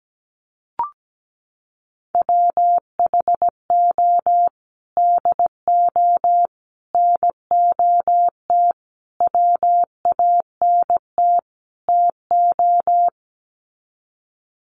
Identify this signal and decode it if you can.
Morse code, heard as a keyed tone, about 17 wpm, 700 Hz. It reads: WHO DO NOT WANT TO
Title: Sets of 5 Words Encore2 17wpm